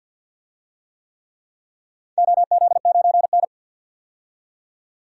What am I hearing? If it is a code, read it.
Morse code, heard as a keyed tone, about 52 wpm, 700 Hz. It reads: OZ9N
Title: DXCC-100 Call Signs - 4 Characters - 50wpm